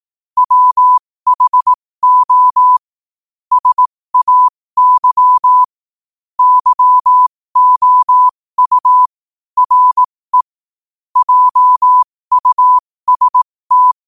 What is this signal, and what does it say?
Morse code, heard as a keyed tone, about 18 wpm, spaced as for 15 wpm, 1000 Hz. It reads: WHO SAY YOU RE JUST
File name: qotdfiles / 15wpm